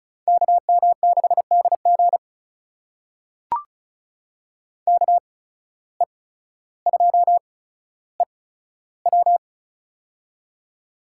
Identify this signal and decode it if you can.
Morse code, heard as a keyed tone, about 36 wpm, spaced as for 15 wpm, 700 Hz. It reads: KM6DG KE2EW
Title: USA Call Signs - 5 Characters - 8x Character Spacing - 35wpm